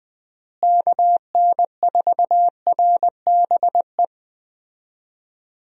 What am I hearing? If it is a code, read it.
Morse code, heard as a keyed tone, about 21 wpm, 700 Hz. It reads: KN4RBE
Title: USA Call Signs - 6 Characters - Repeated 3x - 20wpm